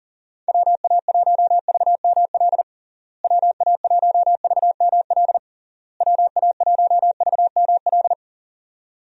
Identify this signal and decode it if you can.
Morse code, heard as a keyed tone, about 40 wpm, 700 Hz. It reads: WA1VML WA1VML WA1VML
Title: USA Call Signs - 6 Characters - Repeated 3x - 40wpm